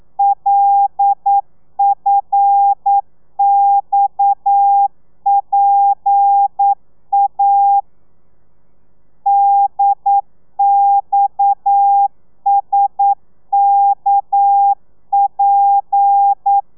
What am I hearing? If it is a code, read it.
Morse code, heard as a keyed tone, about 9 wpm, 800 Hz. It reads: LFXPA DXSKP
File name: telegrafia / i2tfj / Lez7 40